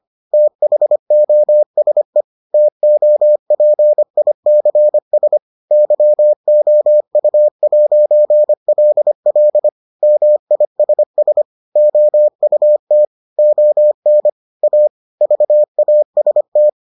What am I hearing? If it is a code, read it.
Morse code, heard as a keyed tone, about 25 wpm, 600 Hz. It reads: THOSE TOPICS YOU'LL MISS OUT ON A VAST